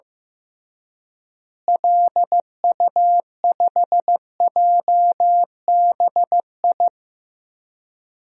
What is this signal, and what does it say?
Morse code, heard as a keyed tone, about 15 wpm, 700 Hz. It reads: LU5JBI
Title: DXCC-100 Call Signs - 6 Characters - Repeated 3x - 15wpm